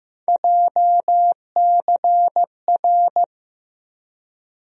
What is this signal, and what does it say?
Morse code, heard as a keyed tone, about 15 wpm, 700 Hz. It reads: JCR
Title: USA Call Sign Postfixes - 3 Letters 15wpm